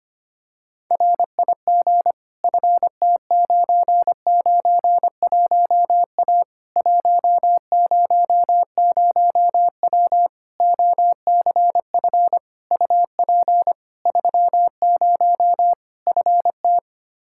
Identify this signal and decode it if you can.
Morse code, heard as a keyed tone, about 25 wpm, 700 Hz. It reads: RIG FT991A 100W OCF UP 30 FT
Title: Practice QSO - Exchange 2 - Info - 25wpm